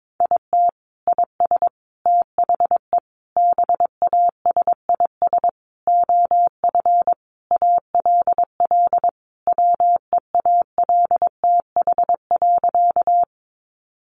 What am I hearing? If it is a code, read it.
Morse code, heard as a keyed tone, about 22 wpm, 700 Hz. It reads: IT IS THE BASIS OF ALL WEALTH.
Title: Sentences from Top 1200 Words 22wpm